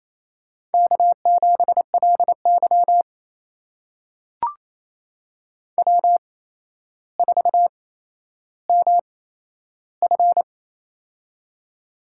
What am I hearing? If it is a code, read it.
Morse code, heard as a keyed tone, about 28 wpm, spaced as for 12 wpm, 700 Hz. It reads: K7LY W4MF
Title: USA Call Signs - 4 Characters - 8x Character Spacing - 28wpm